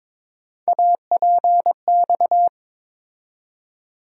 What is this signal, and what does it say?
Morse code, heard as a keyed tone, about 22 wpm, 700 Hz. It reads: APX
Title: USA Call Sign Postfixes - 3 Letters 22wpm